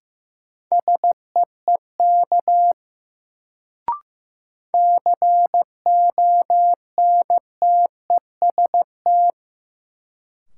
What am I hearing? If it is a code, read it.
Morse code, heard as a keyed tone, about 15 wpm, 700 Hz. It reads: SEEK CONTEST
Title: Sentences from Top 1600 Words - New Word Drill - 15wpm